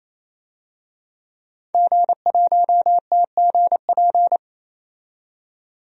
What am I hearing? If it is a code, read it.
Morse code, heard as a keyed tone, about 27 wpm, 700 Hz. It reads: G1TGP
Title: DXCC-100 Call Signs - 5 Characters - 28wpm